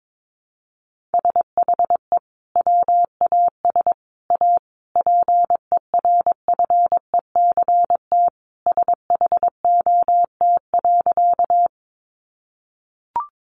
Morse code, heard as a keyed tone, about 22 wpm, 700 Hz. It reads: SHE WAS A PERFECT SHOT.
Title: Sentences from Top 1100 Words 22wpm